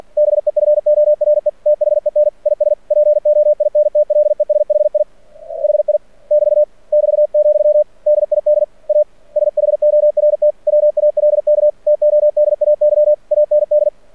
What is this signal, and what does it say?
Morse code, heard as a keyed tone, about 48 wpm, 600 Hz. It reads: BEFORE THEN IS POINTLESHI HI <BT> <BT>, DID A SHORT WALK TODAY AND
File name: archived / CRSnet